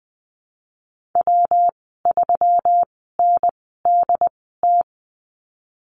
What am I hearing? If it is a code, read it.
Morse code, heard as a keyed tone, about 19 wpm, spaced as for 14 wpm, 700 Hz. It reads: W3NDT
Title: 154 Lesson 25g - Letter p - Call Signs 20wpm